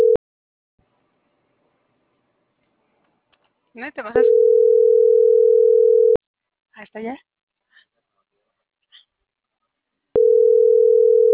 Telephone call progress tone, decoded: ringback tone

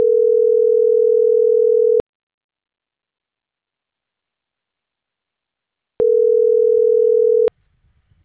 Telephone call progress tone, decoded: ringback tone

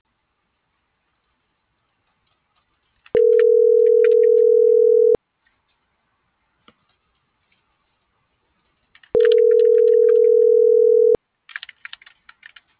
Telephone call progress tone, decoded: ringback tone

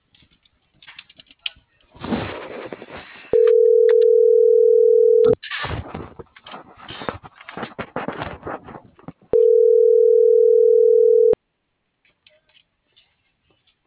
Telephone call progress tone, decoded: ringback tone

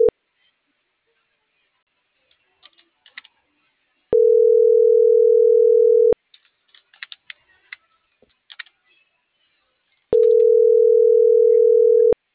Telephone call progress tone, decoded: ringback tone

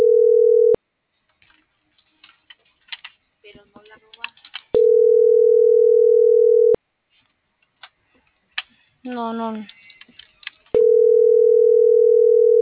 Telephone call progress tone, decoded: ringback tone

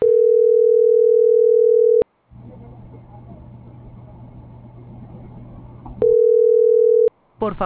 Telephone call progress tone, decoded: ringback tone

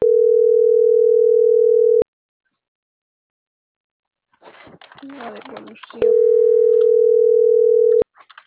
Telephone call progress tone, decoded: ringback tone

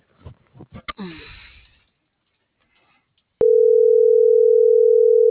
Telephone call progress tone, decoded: ringback tone

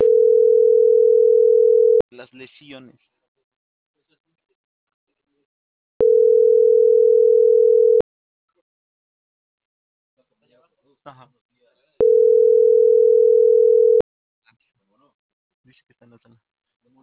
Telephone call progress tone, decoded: ringback tone

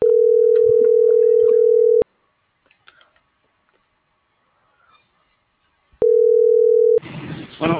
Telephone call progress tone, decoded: ringback tone

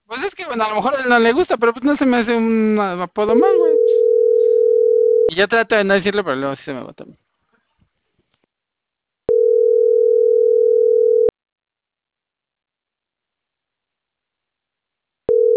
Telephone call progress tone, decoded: ringback tone

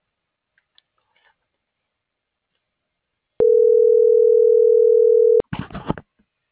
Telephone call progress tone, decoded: ringback tone

